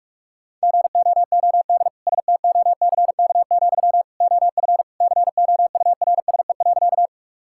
Morse code, heard as a keyed tone, about 43 wpm, 690 Hz. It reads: GOOD STOCK, OF COURSE.
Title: Sentences from Top 1000 Words - Review Part 5 - 45wpm